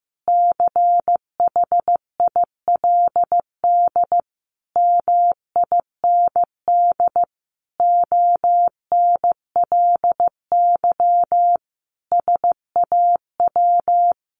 Morse code, heard as a keyed tone, about 15 wpm, 700 Hz. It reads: CHILD MIND ONLY SAW
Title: Sentences from Top 800 Words 15wpm